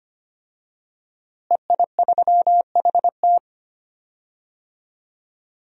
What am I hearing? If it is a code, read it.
Morse code, heard as a keyed tone, about 25 wpm, 700 Hz. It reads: EI3HT
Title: DXCC-100 Call Signs - 5 Characters - Repeated 3x - 25wpm